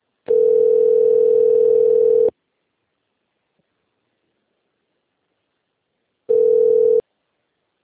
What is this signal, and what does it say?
Ringback tone (North America), 440 + 480 Hz.